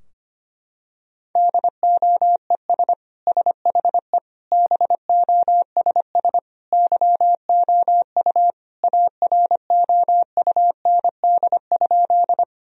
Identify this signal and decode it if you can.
Morse code, heard as a keyed tone, about 25 wpm, 700 Hz. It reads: DOES SHE BOSS YOU AROUND?
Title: Sentences from Top 1000 Words - Review Part 4 - 25wpm